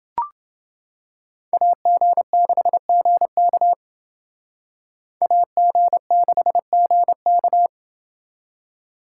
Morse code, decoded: AG6GK AG6GK

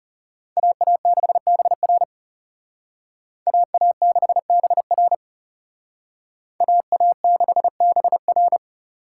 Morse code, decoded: AA6BR AA6BR AA6BR

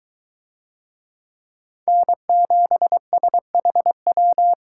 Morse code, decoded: N7SHW

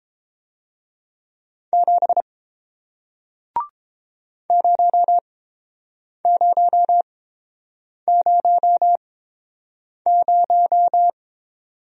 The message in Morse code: 7 0000

